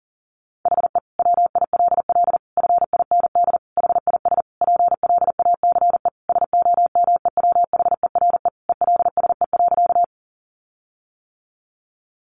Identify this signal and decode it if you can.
Morse code, heard as a keyed tone, about 40 wpm, 700 Hz. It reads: HE WILL FIND HIS PLACE SOMEWHERE ELSE.
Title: Sentences from Top 1400 Words 40wpm